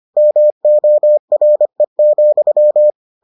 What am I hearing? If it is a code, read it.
Morse code, heard as a keyed tone, about 25 wpm, 600 Hz. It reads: MORE,